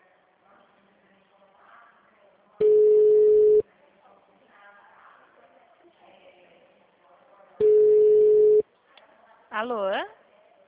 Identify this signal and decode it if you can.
Ringback tone (Europe, 425 Hz), with speech.
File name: out- - - - - .5396